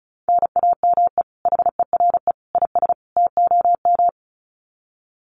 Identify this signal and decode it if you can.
Morse code, heard as a keyed tone, about 34 wpm, 700 Hz. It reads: NAME HERE IS TOM
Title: Common QSO Phrases 35wpm